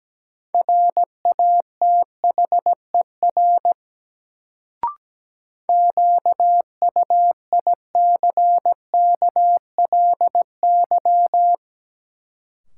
Morse code, heard as a keyed tone, about 17 wpm, 700 Hz. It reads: RATHER QUICKLY
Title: Sentences from Top 700 Words - New Word Drill - 17wpm